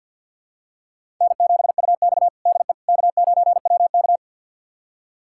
Morse code, heard as a keyed tone, about 52 wpm, 700 Hz. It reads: N7UX DE K9WK
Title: USA Callsign DE Callsign - 4 Characters - 50wpm